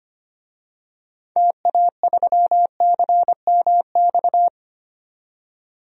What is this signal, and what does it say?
Morse code, heard as a keyed tone, about 25 wpm, 700 Hz. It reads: TA3CMX
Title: DXCC-100 Call Signs - 6 Characters - 25wpm